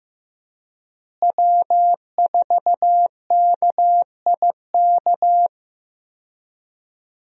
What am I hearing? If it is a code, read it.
Morse code, heard as a keyed tone, about 15 wpm, 700 Hz. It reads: W4KIK
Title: USA Call Signs - 5 Characters - Repeated 2x - 15wpm